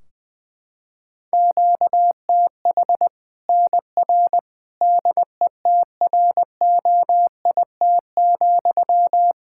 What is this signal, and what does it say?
Morse code, heard as a keyed tone, about 20 wpm, 700 Hz. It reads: QTH NR DETROIT,
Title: Practice QSO - Exchange 1 - Beginning - 20wpm